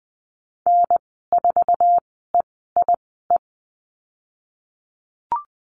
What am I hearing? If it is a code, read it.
Morse code, heard as a keyed tone, about 20 wpm, spaced as for 15 wpm, 700 Hz. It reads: N4EIE
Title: 055 Lesson 10g - Letter r - Call Signs 20wpm